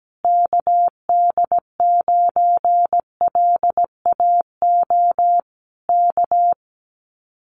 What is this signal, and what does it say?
Morse code, heard as a keyed tone, about 17 wpm, 700 Hz. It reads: KD9LAO K
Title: Common QSO Phrases Encore 17wpm